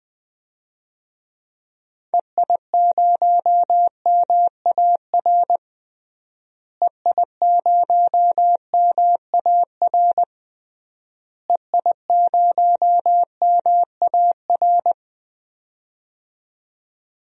Morse code, heard as a keyed tone, about 21 wpm, 700 Hz. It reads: EI0MAR EI0MAR EI0MAR
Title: DXCC-100 Call Signs - 6 Characters - Repeated 3x - 20wpm